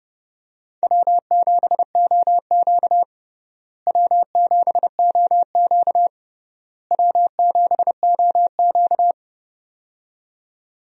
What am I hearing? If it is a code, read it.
Morse code, heard as a keyed tone, about 32 wpm, 700 Hz. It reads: W7OQ W7OQ W7OQ
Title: USA Call Signs - 4 Characters - Repeated 3x - 30wpm